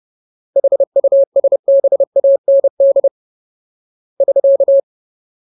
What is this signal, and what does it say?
Morse code, heard as a keyed tone, about 30 wpm, 550 Hz. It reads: HUSBAND <SK>
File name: Ultimate Word List / 30WPM Ultimate